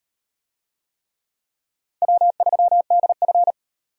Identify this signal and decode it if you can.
Morse code, heard as a keyed tone, about 38 wpm, 700 Hz. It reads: W3DF